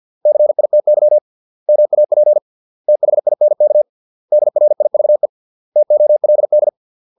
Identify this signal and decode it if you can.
Morse code, heard as a keyed tone, about 50 wpm, 600 Hz. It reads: CITY MAP THINK DRIVE TOLD